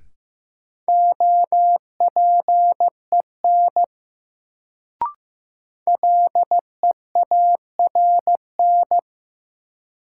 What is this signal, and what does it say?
Morse code, heard as a keyed tone, about 15 wpm, 700 Hz. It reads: OPEN LEARN
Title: Sentences from Top 300 Words - New Word Drill - 15wpm